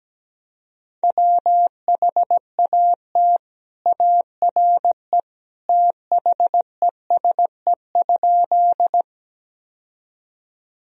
Morse code, heard as a keyed tone, about 17 wpm, 700 Hz. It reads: WHAT ARE THESE?
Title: Sentences from Top 1000 Words - Review Part 1 - 17wpm